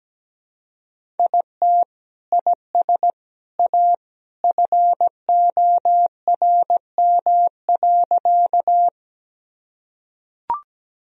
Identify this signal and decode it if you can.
Morse code, heard as a keyed tone, about 17 wpm, 700 Hz. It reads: IT IS A FORM.
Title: Sentences from Top 1000 Words - Review Part 1 - 17wpm